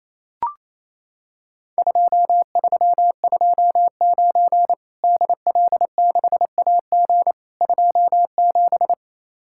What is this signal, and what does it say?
Morse code, heard as a keyed tone, about 28 wpm, 700 Hz. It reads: 2329 DL6AG 27
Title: Worked All Europe DX Contest - QTC Exchange - 28wpm